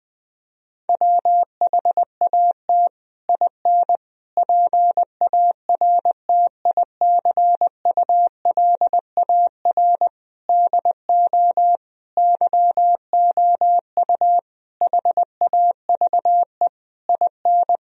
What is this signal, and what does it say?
Morse code, heard as a keyed tone, about 20 wpm, 700 Hz. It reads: WHAT IN PARTICULAR DO YOU HAVE IN
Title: Sentences from Top 1000 Words - Review Part 4 - 20wpm